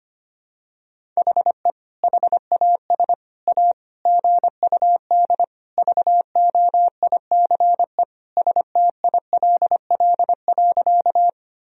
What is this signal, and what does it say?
Morse code, heard as a keyed tone, about 25 wpm, 700 Hz. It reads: HE HAS A GUD VOICE STILL.
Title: Sentences from Top 500 Words and Abbreviations 25wpm